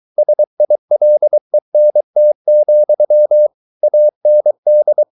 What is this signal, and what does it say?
Morse code, heard as a keyed tone, about 23 wpm, 600 Hz. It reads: SILENT, AND